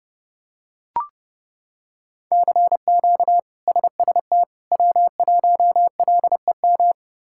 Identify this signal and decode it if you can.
Morse code, heard as a keyed tone, about 30 wpm, 700 Hz. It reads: CQ SST W1LEM
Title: K1USN SST QSO Practice - 30wpm